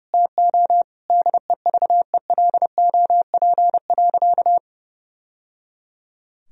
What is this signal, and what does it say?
Morse code, heard as a keyed tone, about 30 wpm, 700 Hz. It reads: TO DEVELOP.
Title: Sentences from Top 1000 Words - Review Part 2 - 30wpm